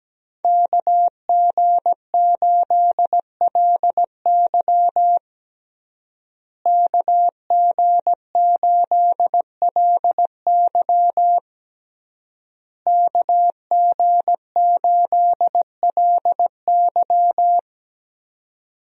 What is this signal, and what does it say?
Morse code, heard as a keyed tone, about 17 wpm, 700 Hz. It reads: KG8LY KG8LY KG8LY
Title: USA Call Signs - 5 Characters - Repeated 3x - 17wpm